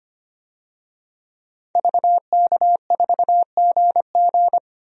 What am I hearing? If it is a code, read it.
Morse code, heard as a keyed tone, about 25 wpm, 700 Hz. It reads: VK4GG
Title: DXCC-100 Call Signs - 5 Characters - 25wpm